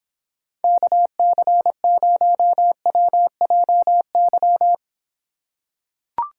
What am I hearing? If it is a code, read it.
Morse code, heard as a keyed tone, about 26 wpm, 700 Hz. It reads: KC0WJY